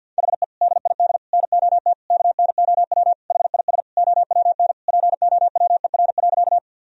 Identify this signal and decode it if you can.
Morse code, heard as a keyed tone, about 51 wpm, 700 Hz. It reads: HE DID NOT KNOW HIS OWN POWER.
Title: Sentences from Top 1000 Words - Review Part 2 - 50wpm